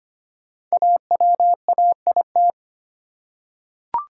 Morse code, heard as a keyed tone, about 25 wpm, 700 Hz. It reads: AWAIT
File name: 5 Letter Words - First 3 Letters Given - 25wpm